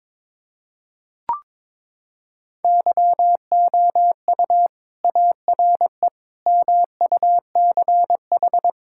Morse code, heard as a keyed tone, about 22 wpm, 700 Hz. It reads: YOU ARE MUCH
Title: Sentences from Top 1000 Words - Review Part 5 - 22wpm